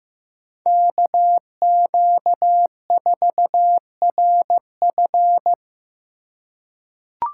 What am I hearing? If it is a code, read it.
Morse code, heard as a keyed tone, about 16 wpm, 700 Hz. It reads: KQ4RF
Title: USA Call Signs - 5 Characters - 4x Character Spacing - 15wpm